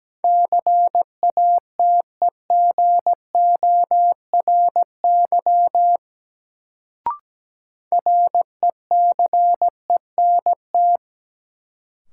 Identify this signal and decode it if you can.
Morse code, heard as a keyed tone, about 17 wpm, 700 Hz. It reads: CATEGORY RECENT
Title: Sentences from Top 1000 Words - Review Part 7 - New Word Drill - 17wpm